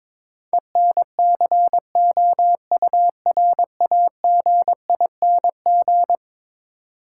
Morse code, heard as a keyed tone, about 22 wpm, 700 Hz. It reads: ENCOURAGING
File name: practice / 22wpm / Eleven Letter Words - 22wpm